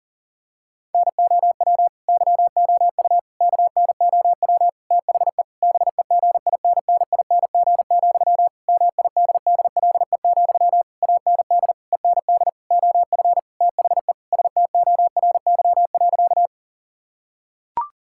Morse code, heard as a keyed tone, about 40 wpm, 700 Hz. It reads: NOW YOU KNOW THE BEGINNING, MIDDLE, AND END OF THE STORY.
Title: Sentences from Top 1000 Words - Review Part 6 - 40wpm